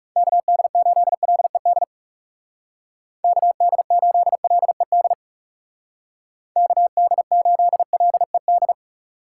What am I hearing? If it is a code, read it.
Morse code, heard as a keyed tone, about 37 wpm, 700 Hz. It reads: KD8LED KD8LED KD8LED
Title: Speed Racing - USA 2x3 Call Signs 30wpm